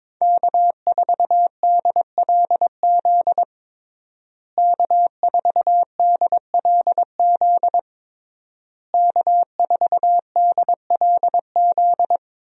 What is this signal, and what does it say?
Morse code, heard as a keyed tone, about 21 wpm, 700 Hz. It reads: K4DLZ K4DLZ K4DLZ
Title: USA Call Signs - 5 Characters - Repeated 3x - 22wpm